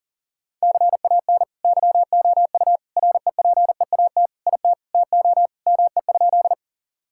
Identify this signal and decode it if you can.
Morse code, heard as a keyed tone, about 40 wpm, 700 Hz. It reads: CAN YOU REPEAT IT TO ME?
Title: Sentences from Top 1600 Words 40wpm